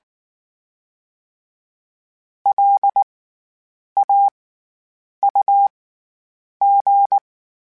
Morse code, heard as a keyed tone, about 19 wpm, spaced as for 8 wpm, 800 Hz. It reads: LAUG
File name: Special INGs / Special INGs-mp3s / SlowToFaster ing @18